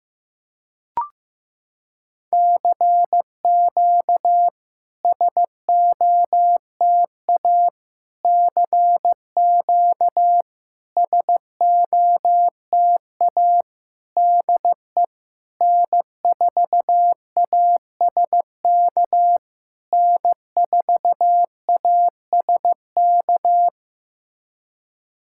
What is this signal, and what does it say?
Morse code, heard as a keyed tone, about 15 wpm, 690 Hz. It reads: CQ SOTA CQ SOTA DE N4ASK N4ASK
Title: SOTA QSO Practice 15wpm